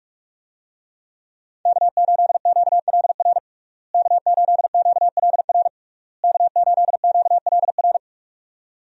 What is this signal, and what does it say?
Morse code, heard as a keyed tone, about 41 wpm, 700 Hz. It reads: K8QLR K8QLR K8QLR
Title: USA Call Signs - 5 Characters - Repeated 3x - 45wpm